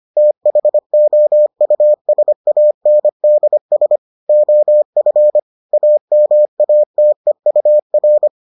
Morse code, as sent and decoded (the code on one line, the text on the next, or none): - .... --- ..- ... .- -. -.. ... / --- ..-. / .- -- .- - . ..- .-.
THOUSANDS OF AMATEUR